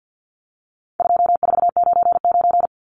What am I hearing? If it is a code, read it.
Morse code, heard as a keyed tone, about 50 wpm, 700 Hz. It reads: 2499
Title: Numbers - 4-Digit 50wpm